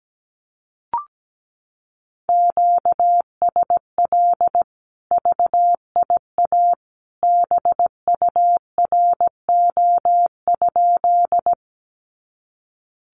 Morse code, heard as a keyed tone, about 17 wpm, 700 Hz. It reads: QSL VIA BURO?
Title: Common QSO Phrases 17wpm